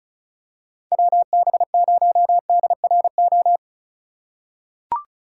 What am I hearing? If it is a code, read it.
Morse code, heard as a keyed tone, about 34 wpm, 690 Hz. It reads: WB0DRO
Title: USA Call Signs - 6 Characters - 4x Character Spacing - 35wpm